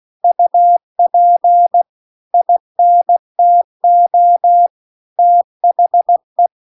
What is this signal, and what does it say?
Morse code, heard as a keyed tone, about 16 wpm, 700 Hz. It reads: UP INTO THE
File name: CW-The-Big-Twoheaded-River-2-16WPM